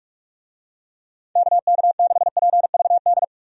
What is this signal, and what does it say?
Morse code, heard as a keyed tone, about 47 wpm, 700 Hz. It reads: KK6PVD
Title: USA Call Signs - 6 Characters - 45wpm